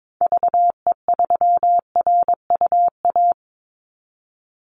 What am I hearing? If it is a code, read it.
Morse code, heard as a keyed tone, about 22 wpm, 700 Hz. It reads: VE3RUA
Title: Call Signs - 22wpm